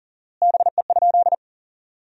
Morse code, heard as a keyed tone, about 40 wpm, 700 Hz. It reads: BE?
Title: 4x Word-Spacing - Sentences from Top 300 Words - 40wpm